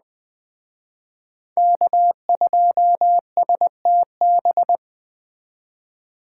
Morse code, heard as a keyed tone, about 20 wpm, 700 Hz. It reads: K2STB
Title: Speed Racing - USA 1x3 Call Signs 17wpm